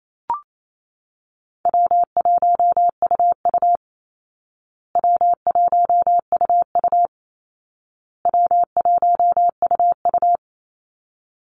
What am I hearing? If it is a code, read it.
Morse code, heard as a keyed tone, about 28 wpm, 700 Hz. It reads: W1UU W1UU W1UU
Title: Call Signs Repeated 2x - 28wpm